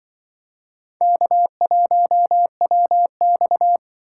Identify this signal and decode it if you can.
Morse code, heard as a keyed tone, about 25 wpm, 700 Hz. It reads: K1WX
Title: Speed Racing - USA 1x2 Call Signs 20wpm